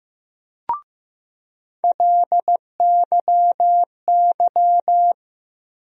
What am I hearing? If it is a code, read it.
Morse code, heard as a keyed tone, about 15 wpm, 700 Hz. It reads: LYY
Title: USA Call Sign Postfixes - 3 Letters 15wpm